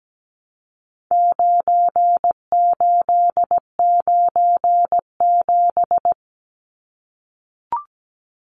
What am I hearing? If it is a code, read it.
Morse code, heard as a keyed tone, about 17 wpm, 700 Hz. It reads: 9897
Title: Numbers - 4-Digit 17wpm